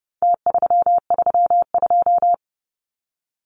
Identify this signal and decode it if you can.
Morse code, heard as a keyed tone, about 30 wpm, 700 Hz. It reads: T332
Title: Numbers - Contest Style - 4-Digit 30wpm